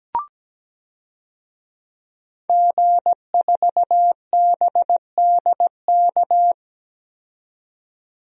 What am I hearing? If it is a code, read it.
Morse code, heard as a keyed tone, about 17 wpm, 700 Hz. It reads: G4BDK
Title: DXCC-100 Call Signs - 5 Characters - 17wpm